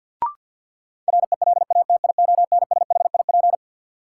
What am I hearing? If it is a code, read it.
Morse code, heard as a keyed tone, about 52 wpm, 700 Hz. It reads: RELATIONSHIP
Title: Sentences from Top 1100 Words - New Word Drill - 50wpm